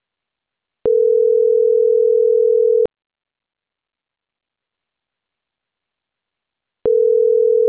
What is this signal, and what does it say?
Ringback tone (North America), 440 + 480 Hz.